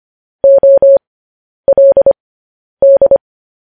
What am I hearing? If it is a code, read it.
Morse code, heard as a keyed tone, about 25 wpm, spaced as for 10 wpm, 550 Hz. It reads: OLD
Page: News Headlines in Morse Code at 10 WPM